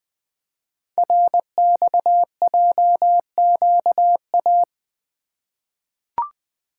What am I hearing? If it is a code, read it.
Morse code, heard as a keyed tone, about 20 wpm, 700 Hz. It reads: RXJQA
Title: Code Groups - 5 Letters - 20wpm